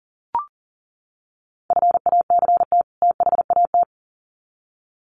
Morse code, heard as a keyed tone, about 39 wpm, 700 Hz. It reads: FACT THAT
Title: Sets of 2 Words 40wpm